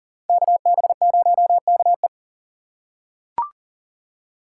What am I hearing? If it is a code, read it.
Morse code, heard as a keyed tone, about 40 wpm, 700 Hz. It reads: KB0KE
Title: USA Call Signs - 5 Characters - 2x Character Spacing - 40wpm